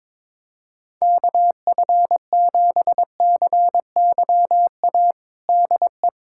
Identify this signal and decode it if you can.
Morse code, heard as a keyed tone, about 22 wpm, 700 Hz. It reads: KF7CYA DE
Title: USA Callsign DE Callsign - 6 Characters - 22wpm